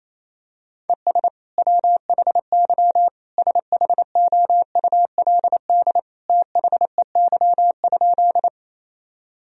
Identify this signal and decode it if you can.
Morse code, heard as a keyed tone, about 27 wpm, 700 Hz. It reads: ES WHY SHOULD THEY?
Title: Sentences from Top 300 Words and Abbreviations 28wpm